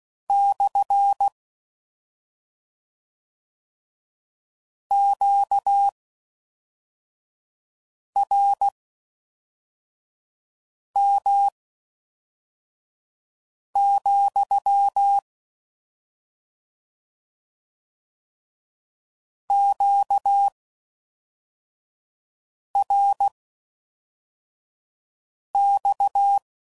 Morse code, heard as a keyed tone, about 16 wpm, spaced as for 4 wpm, 800 Hz. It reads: / QRM, QRX